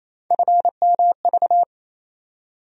Morse code, heard as a keyed tone, about 28 wpm, 700 Hz. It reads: FMV
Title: USA Call Sign Postfixes - 3 Letters 28wpm